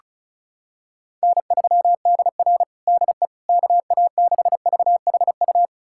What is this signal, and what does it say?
Morse code, heard as a keyed tone, about 36 wpm, 700 Hz. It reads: N3DR DE KA6VHU
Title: USA Callsign DE Callsign 35wpm